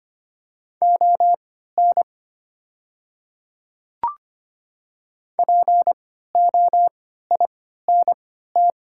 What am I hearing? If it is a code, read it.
Morse code, heard as a keyed tone, about 25 wpm, spaced as for 14 wpm, 700 Hz. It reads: ON POINT
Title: 3x Character Spacing - Top 200 Words - 25wpm